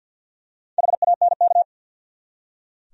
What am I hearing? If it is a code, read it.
Morse code, heard as a keyed tone, about 46 wpm, 700 Hz. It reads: HANK
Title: USA Common Names - 50wpm